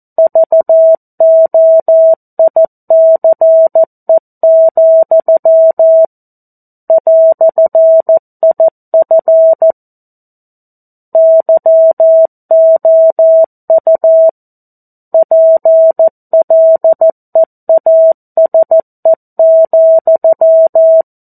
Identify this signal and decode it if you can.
Morse code, heard as a keyed tone, about 14 wpm, 650 Hz. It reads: VOICE, "IF YOU PLEASE,
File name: 02-Chapter-02 14 WPM